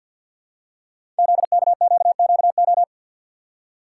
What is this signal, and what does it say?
Morse code, heard as a keyed tone, about 46 wpm, 700 Hz. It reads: ZKQQO
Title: Code Groups - 5 Letters - 50wpm